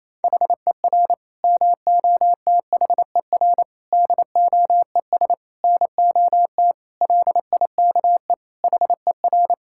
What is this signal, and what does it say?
Morse code, heard as a keyed tone, about 27 wpm, 700 Hz. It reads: HER MOTHER DOES NOT LIKE HER
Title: Sentences from Top 1000 Words - Review Part 5 - 28wpm